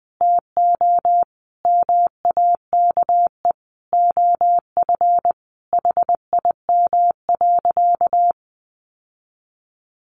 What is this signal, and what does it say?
Morse code, heard as a keyed tone, about 20 wpm, 700 Hz. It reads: TO MAKE OF HIM.
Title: Sentences from Top 1400 Words 20wpm